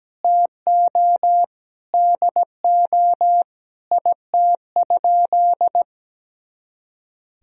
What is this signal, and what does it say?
Morse code, heard as a keyed tone, about 17 wpm, 700 Hz. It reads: TO DO IT?
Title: Sentences from Top 200 Words and Abbreviations 17wpm